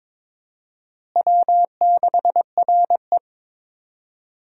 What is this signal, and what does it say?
Morse code, heard as a keyed tone, about 22 wpm, 700 Hz. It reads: W6RE
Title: USA Call Signs - 4 Characters - Repeated 3x - 22wpm